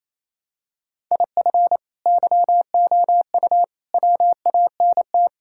Morse code, heard as a keyed tone, about 28 wpm, 700 Hz. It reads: IF YOU WANT
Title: Sentences from Top 1000 Words - Review Part 6 - 28wpm